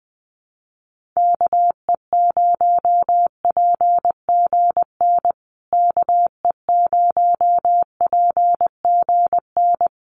Morse code, heard as a keyed tone, about 20 wpm, 700 Hz. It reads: KE0PGN KE0PGN
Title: USA Call Signs Repeated 3x 20wpm